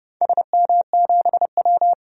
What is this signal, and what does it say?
Morse code, heard as a keyed tone, about 30 wpm, 700 Hz. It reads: SM7W
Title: DXCC-100 Call Signs - 4 Characters - 30wpm